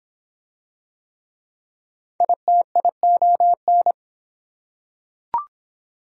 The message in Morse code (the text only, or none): ITION